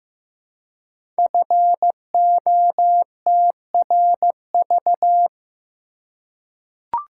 Morse code, heard as a keyed tone, about 15 wpm, 700 Hz. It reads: FOTRV
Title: Code Groups - 5 Letters - 15wpm